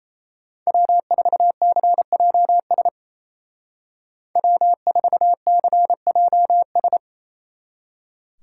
Morse code, heard as a keyed tone, about 36 wpm, 700 Hz. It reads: W4CJS W4CJS